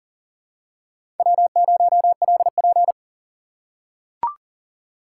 Morse code, heard as a keyed tone, about 40 wpm, 700 Hz. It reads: W0LP
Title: USA Call Signs - 4 Characters - 2x Character Spacing - 40wpm